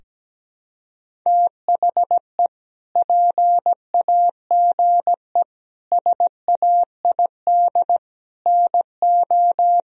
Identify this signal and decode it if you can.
Morse code, heard as a keyed tone, about 17 wpm, 700 Hz. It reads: THE PAGE SAID NO
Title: Sentences from Top 1000 Words - Review Part 1 - 17wpm